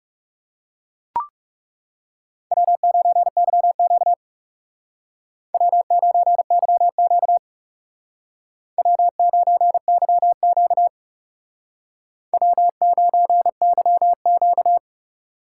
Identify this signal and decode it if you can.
Morse code, heard as a keyed tone, about 38 wpm, 700 Hz. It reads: W9YQ W9YQ W9YQ W9YQ